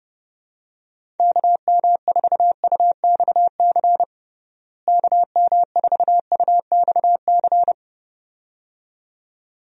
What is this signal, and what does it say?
Morse code, heard as a keyed tone, about 30 wpm, 700 Hz. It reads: KM4UXC KM4UXC
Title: USA Call Signs - 6 Characters - Repeated 2x - 30wpm